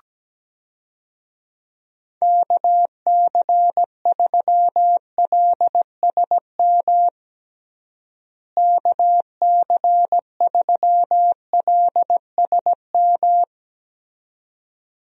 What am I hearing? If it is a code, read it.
Morse code, heard as a keyed tone, about 17 wpm, 700 Hz. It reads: KC3LSM KC3LSM
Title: USA Call Signs - 6 Characters - Repeated 2x - 17wpm